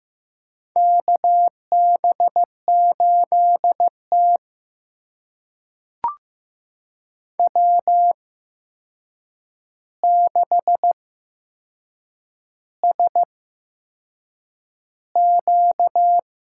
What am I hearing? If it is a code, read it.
Morse code, heard as a keyed tone, about 15 wpm, spaced as for 6 wpm, 700 Hz. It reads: KB8T W6SQ
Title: USA Call Signs - 4 Characters - 8x Character Spacing - 15wpm